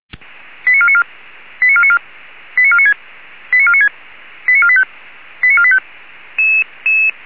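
Five-tone selective call (ZVEI), heard as quick runs of five tones, tones of about 70 ms. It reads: 89494 89494 89496 89496 89485 89485
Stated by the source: Five-tone ZVEI selective call received by SDR, logged on the Feuerwehr Orlamünde audio server.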